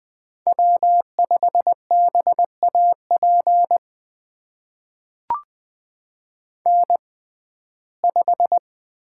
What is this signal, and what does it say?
Morse code, heard as a keyed tone, about 20 wpm, spaced as for 14 wpm, 700 Hz. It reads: W5BAP N5
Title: USA Call Signs - 5 Characters - 6x Character Spacing - 20wpm